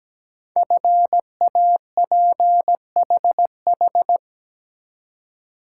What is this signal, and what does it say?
Morse code, heard as a keyed tone, about 17 wpm, 700 Hz. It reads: FAPHH
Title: Code Groups - 5 Letters - 17wpm